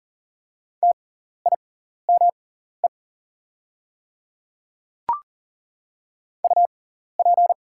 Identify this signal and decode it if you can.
Morse code, heard as a keyed tone, about 39 wpm, spaced as for 14 wpm, 690 Hz. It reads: TIME UP
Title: 6x Character Spacing - Top 100 Words - 40wpm